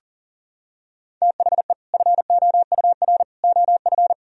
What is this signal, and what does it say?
Morse code, heard as a keyed tone, about 40 wpm, 700 Hz. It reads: THE FOUR OF
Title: Sentences from Top 1000 Words - Review Part 1 - 40wpm